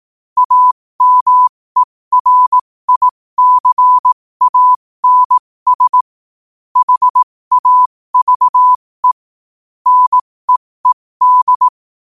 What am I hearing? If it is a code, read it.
Morse code, heard as a keyed tone, about 18 wpm, spaced as for 15 wpm, 1000 Hz. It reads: AMERICANS HAVE NEED
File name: 15wpm